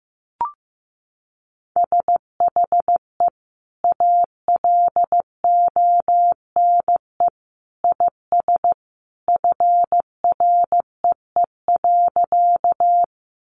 Morse code, heard as a keyed tone, about 15 wpm, 700 Hz. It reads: SHE ALONE IS FREE.
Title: Sentences from Top 1000 Words 15wpm